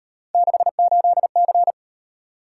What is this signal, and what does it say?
Morse code, heard as a keyed tone, about 38 wpm, 700 Hz. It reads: 68C